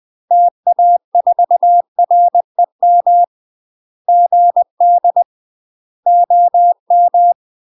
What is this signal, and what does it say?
Morse code, heard as a keyed tone, about 20 wpm, 700 Hz. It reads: TA4REM GD OM
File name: UK MASTER QSOs TRAINING / 20WPM / UK MASTER QSOs 0048